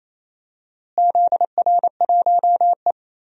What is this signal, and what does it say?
Morse code, heard as a keyed tone, about 28 wpm, 700 Hz. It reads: ZR1E
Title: DXCC-100 Call Signs - 4 Characters - Repeated 2x - 28wpm